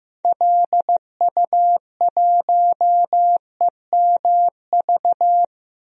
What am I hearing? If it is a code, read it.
Morse code, heard as a keyed tone, about 15 wpm, 700 Hz. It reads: LU1EMV
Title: DXCC-100 Call Signs - 6 Characters - Repeated 2x - 15wpm